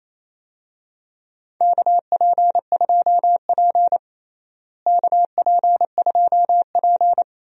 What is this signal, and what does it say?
Morse code, heard as a keyed tone, about 28 wpm, 700 Hz. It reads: KP2P KP2P
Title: DXCC-100 Call Signs - 4 Characters - Repeated 2x - 28wpm